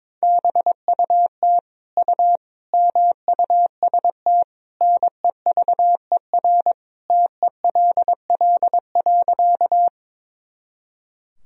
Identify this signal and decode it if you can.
Morse code, heard as a keyed tone, about 22 wpm, 700 Hz. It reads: BUT U MUST NEVER TELL.
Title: Sentences from Top 300 Words and Abbreviations 22wpm